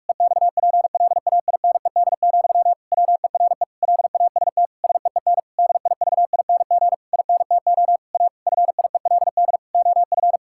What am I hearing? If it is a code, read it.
Morse code, heard as a keyed tone, about 46 wpm, 700 Hz. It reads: EXPLAINED, WERE LAST SEEN DIVING INTO A FIELD OF
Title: Pangram Challenge v2 - 45wpm